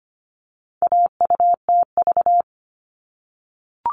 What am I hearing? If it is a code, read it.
Morse code, heard as a keyed tone, about 25 wpm, 700 Hz. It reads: AUTV
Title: All Cut Numbers - 4-Digit 25wpm